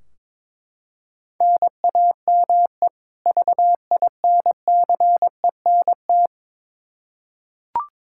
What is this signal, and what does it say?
Morse code, heard as a keyed tone, about 22 wpm, 700 Hz. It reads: NAME VINCENT
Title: Practice QSO - Exchange 1 - Beginning - 22wpm